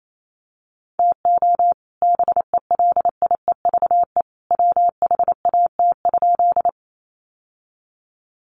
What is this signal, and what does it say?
Morse code, heard as a keyed tone, about 28 wpm, 700 Hz. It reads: TO BELIEVE WHAT?
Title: Sentences from Top 1000 Words 28wpm